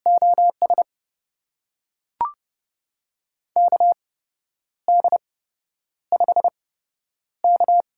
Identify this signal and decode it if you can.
Morse code, heard as a keyed tone, about 30 wpm, spaced as for 10 wpm, 700 Hz. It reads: OS KD5K
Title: USA Call Signs - 6 Characters - 8x Character Spacing - 30wpm